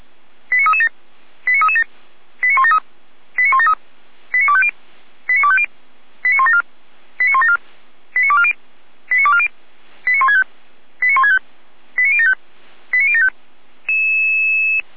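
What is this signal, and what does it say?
Five-tone selective call (ZVEI), heard as quick runs of five tones, tones of about 70 ms. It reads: 89337 89337 89173 89173 89250 89250 89174 89174 89240 89240 89165 89165 89075 89075
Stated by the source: ZVEI five-tone selective call received by SDR, feuerwehr-orlamuende.org (fire brigade Orlamünde) recording.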